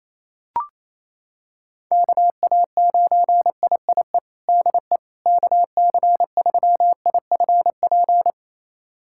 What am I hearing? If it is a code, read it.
Morse code, heard as a keyed tone, about 28 wpm, 700 Hz. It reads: KA9IIE DE KC3IFP
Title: USA Callsign DE Callsign 28wpm